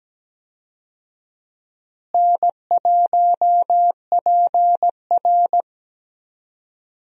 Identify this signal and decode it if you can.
Morse code, heard as a keyed tone, about 17 wpm, 700 Hz. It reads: N1PR
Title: USA Call Signs - 4 Characters - Repeated 2x - 17wpm